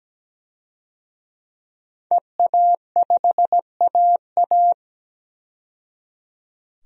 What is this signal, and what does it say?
Morse code, heard as a keyed tone, about 17 wpm, 700 Hz. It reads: EA5AA
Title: DXCC-100 Call Signs - 5 Characters - 17wpm